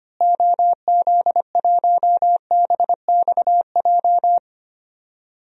Morse code, heard as a keyed tone, about 25 wpm, 700 Hz. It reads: OZ1BXJ
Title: DXCC-100 Call Signs - 6 Characters - Repeated 3x - 25wpm